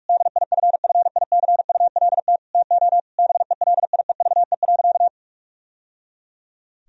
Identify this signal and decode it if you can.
Morse code, heard as a keyed tone, about 46 wpm, 700 Hz. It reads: DIFFICULT TO BELIEVE.
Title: Sentences from Top 1000 Words - Review Part 4 - 45wpm